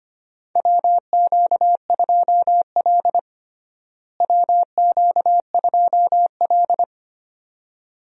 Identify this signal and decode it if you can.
Morse code, heard as a keyed tone, about 25 wpm, 700 Hz. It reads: WQ2L WQ2L
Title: USA Call Signs - 4 Characters - Repeated 2x - 25wpm